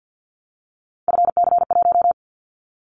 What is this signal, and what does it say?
Morse code, heard as a keyed tone, about 50 wpm, 700 Hz. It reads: FC1
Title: Callsign Prefixes for Top 50 DXCC - 50wpm